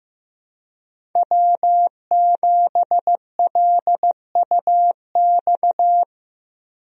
Morse code, heard as a keyed tone, about 15 wpm, 700 Hz. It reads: W7LUX
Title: USA Call Signs - 5 Characters - Repeated 3x - 15wpm